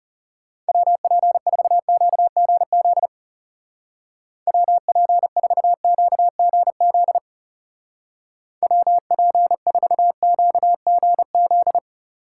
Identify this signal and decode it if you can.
Morse code, heard as a keyed tone, about 36 wpm, 700 Hz. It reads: WP4QGZ WP4QGZ WP4QGZ
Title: Speed Racing - USA Call Signs 30wpm